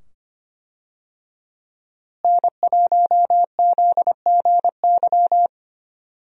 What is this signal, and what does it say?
Morse code, heard as a keyed tone, about 25 wpm, 700 Hz. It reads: N1ZGY
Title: USA Call Signs - 5 Characters - Repeated 2x - 25wpm